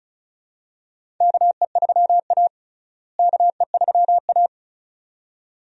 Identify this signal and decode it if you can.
Morse code, heard as a keyed tone, about 36 wpm, 700 Hz. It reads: KE3A KE3A
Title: USA Call Signs - 4 Characters - Repeated 2x - 35wpm